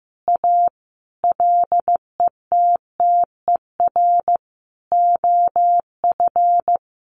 Morse code, heard as a keyed tone, about 15 wpm, 700 Hz. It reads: A LETTER OF
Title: Sentences from Top 1400 Words 15wpm